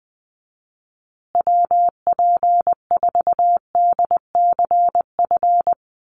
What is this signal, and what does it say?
Morse code, heard as a keyed tone, about 20 wpm, 700 Hz. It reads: WP4DCF
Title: USA Call Signs Repeated 2x 20wpm